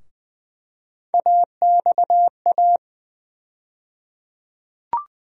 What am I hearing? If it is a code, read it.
Morse code, heard as a keyed tone, about 20 wpm, 700 Hz. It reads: AXA
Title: USA Call Sign Postfixes - 3 Letters 20wpm